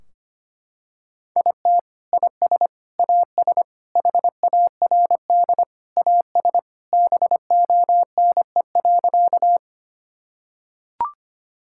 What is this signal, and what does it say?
Morse code, heard as a keyed tone, about 25 wpm, 700 Hz. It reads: IT IS AS HARD AS BONE.
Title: Sentences from Top 1600 Words 25wpm